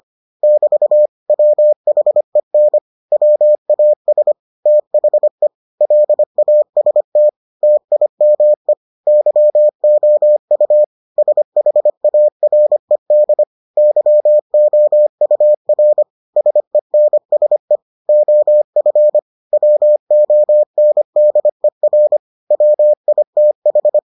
Morse code, heard as a keyed tone, about 25 wpm, 600 Hz. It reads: <BT> WHEN WAS THE LAST TIME YOU SHARED YOUR SENSE OF WONDER WITH